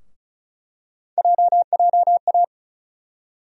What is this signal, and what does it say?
Morse code, heard as a keyed tone, about 36 wpm, 700 Hz. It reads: JJA
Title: USA Call Sign Postfixes - 3 Letters 35wpm